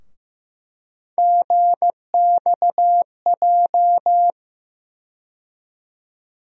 Morse code, heard as a keyed tone, about 15 wpm, 700 Hz. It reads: GXJ
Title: USA Call Sign Postfixes - 3 Letters 15wpm